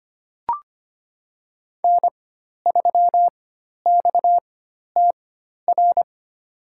Morse code, heard as a keyed tone, about 25 wpm, spaced as for 12 wpm, 700 Hz. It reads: N3XTR
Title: USA Call Signs - 5 Characters - 4x Character Spacing - 25wpm